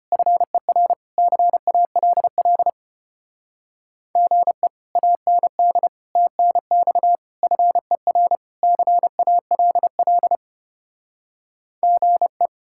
Morse code, heard as a keyed tone, about 33 wpm, 700 Hz. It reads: FER CALL GE AND TNX FER CALL GE